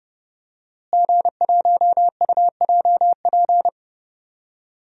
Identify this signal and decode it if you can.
Morse code, heard as a keyed tone, about 31 wpm, 700 Hz. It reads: G1UJP
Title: DXCC-100 Call Signs - 5 Characters - Repeated 3x - 30wpm